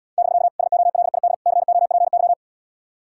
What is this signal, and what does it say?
Morse code, heard as a keyed tone, about 80 wpm, 700 Hz. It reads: <BK> SOLID COPY